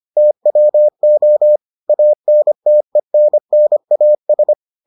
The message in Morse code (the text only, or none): TWO ANTENNAS